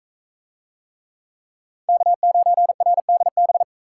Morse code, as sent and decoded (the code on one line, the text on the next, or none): -.- ----. .-. -.. -...
K9RDB